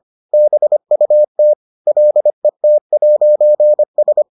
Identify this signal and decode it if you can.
Morse code, heard as a keyed tone, about 25 wpm, 600 Hz. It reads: BUT LET'S